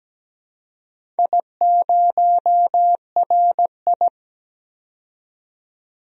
Morse code, heard as a keyed tone, about 17 wpm, 700 Hz. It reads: I0RI
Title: DXCC-100 Call Signs - 4 Characters - 17wpm